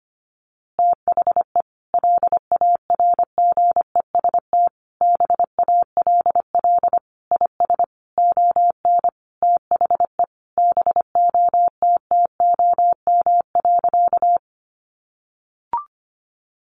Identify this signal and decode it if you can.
Morse code, heard as a keyed tone, about 25 wpm, 700 Hz. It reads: THE LARGEST BALL IS ON THE BOTTOM.
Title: Easy Phrases 25wpm